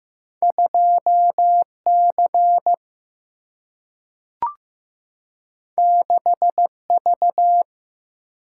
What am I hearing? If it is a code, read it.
Morse code, heard as a keyed tone, about 15 wpm, 700 Hz. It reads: 2C 6V
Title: Single-Letter Single-Number 15wpm